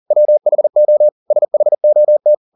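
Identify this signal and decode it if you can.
Morse code, heard as a keyed tone, about 40 wpm, 600 Hz. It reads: WHO SHOT
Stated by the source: Morse code at 40 wpm, practice text, CW med Gnister.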